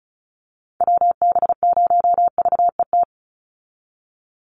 Morse code, heard as a keyed tone, about 34 wpm, 700 Hz. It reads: WB0VET
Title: USA Call Signs 35wpm